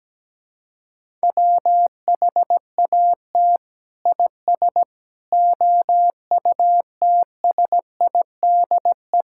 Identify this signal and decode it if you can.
Morse code, heard as a keyed tone, about 17 wpm, 700 Hz. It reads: WHAT IS OUTSIDE
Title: Sentences from Top 1000 Words - Review Part 6 - 17wpm